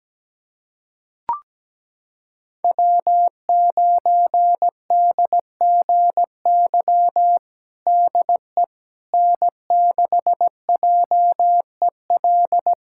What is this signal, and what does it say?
Morse code, heard as a keyed tone, about 17 wpm, 700 Hz. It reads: W9DGY DE N6JEL
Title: USA Callsign DE Callsign - 5 Characters - 17wpm